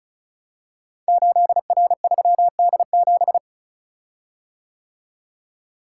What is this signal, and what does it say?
Morse code, heard as a keyed tone, about 36 wpm, 700 Hz. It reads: 8R3D7
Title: Code Groups - 5 Letter:Numbers - 35wpm